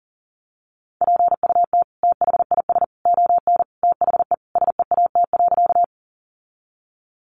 Morse code, heard as a keyed tone, about 40 wpm, 700 Hz. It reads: PUT THIS ON THE SEAT.
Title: Sentences from Top 1500 Words 40wpm